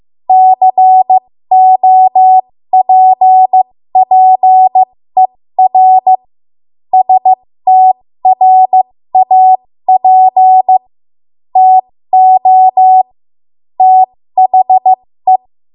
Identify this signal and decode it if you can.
Morse code, heard as a keyed tone, about 15 wpm, spaced as for 13 wpm, 750 Hz. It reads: COPPER STRAP TO THE
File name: Morse / 13WPM